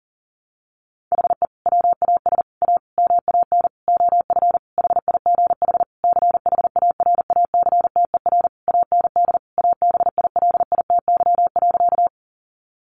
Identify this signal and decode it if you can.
Morse code, heard as a keyed tone, about 40 wpm, 700 Hz. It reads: HE WAS A MAN OF HIGH CHARACTER AND ABILITY.
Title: Sentences from Top 1200 Words 40wpm